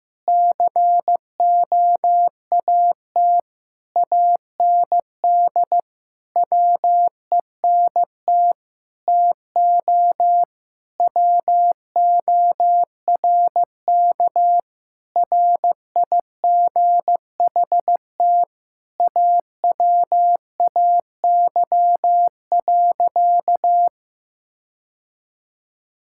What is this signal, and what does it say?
Morse code, heard as a keyed tone, about 15 wpm, 700 Hz. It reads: COAT AND WENT TO WORK RIGHT AWAY.
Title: Sentences from Top 1600 Words 15wpm